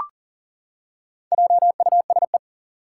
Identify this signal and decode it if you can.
Morse code, heard as a keyed tone, about 40 wpm, 700 Hz. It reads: JUSE
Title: Four Challenging Letter-Number Combinations 40wpm